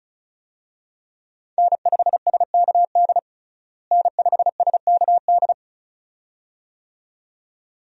Morse code, heard as a keyed tone, about 34 wpm, 700 Hz. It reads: N5SKD N5SKD
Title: USA Call Signs - 5 Characters - Repeated 2x - 35wpm